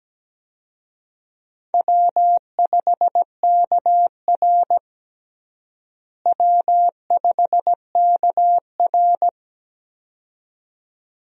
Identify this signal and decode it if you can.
Morse code, heard as a keyed tone, about 17 wpm, 700 Hz. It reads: W5KR W5KR
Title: USA Call Signs - 4 Characters - Repeated 2x - 17wpm